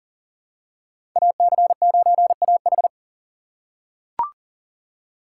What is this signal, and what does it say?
Morse code, heard as a keyed tone, about 40 wpm, 700 Hz. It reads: AC9AH
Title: USA Call Signs - 2x Character Spacing - 40wpm